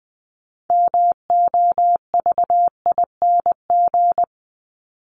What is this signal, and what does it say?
Morse code, heard as a keyed tone, about 20 wpm, 700 Hz. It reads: MOVING
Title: ING Suffix - Top 100 Words with Base Words 20wpm